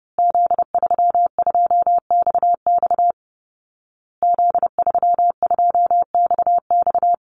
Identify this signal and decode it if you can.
Morse code, heard as a keyed tone, about 28 wpm, 700 Hz. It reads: Z32XX Z32XX
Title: Call Signs Repeated 2x - 30wpm